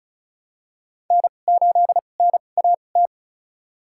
Morse code, heard as a keyed tone, about 36 wpm, spaced as for 25 wpm, 700 Hz. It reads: N8NAT
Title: USA Call Signs - 5 Characters - 2x Character Spacing - 35wpm